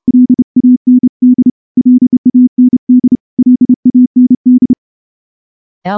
Morse code, heard as a keyed tone, about 29 wpm, 250 Hz. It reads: LAND LAND LAND
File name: morse-training-30WPM